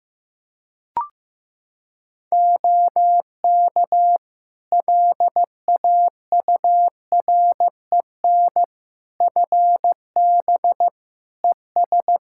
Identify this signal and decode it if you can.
Morse code, heard as a keyed tone, about 16 wpm, 700 Hz. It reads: OK LAUREN FB ES
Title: Practice QSO - Exchange 2 - Info - 15wpm